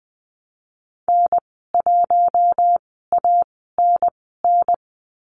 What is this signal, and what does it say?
Morse code, heard as a keyed tone, about 20 wpm, spaced as for 15 wpm, 700 Hz. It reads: N1ANN
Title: 055 Lesson 10g - Letter r - Call Signs 20wpm